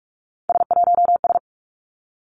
Morse code, heard as a keyed tone, about 46 wpm, 700 Hz. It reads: S1S
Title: Three Challenging Letter-Number Combinations 45wpm